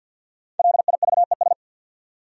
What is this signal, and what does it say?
Morse code, heard as a keyed tone, about 53 wpm, 700 Hz. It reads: LIVES
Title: Sets of 5 Words Encore - New Word Drill - 50wpm